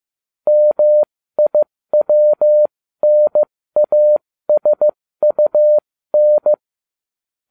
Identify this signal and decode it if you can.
Morse code, heard as a keyed tone, about 15 wpm, spaced as for 13 wpm, 600 Hz. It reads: MIWNASUN